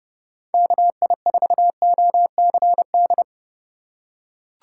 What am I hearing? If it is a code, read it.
Morse code, heard as a keyed tone, about 32 wpm, 700 Hz. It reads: KI4OCD
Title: USA Call Signs - 6 Characters - 4x Character Spacing - 30wpm